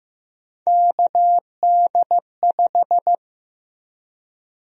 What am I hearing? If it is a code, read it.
Morse code, heard as a keyed tone, about 15 wpm, 700 Hz. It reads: KD5
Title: USA Call Sign Prefixes - Repeated 2x - 15wpm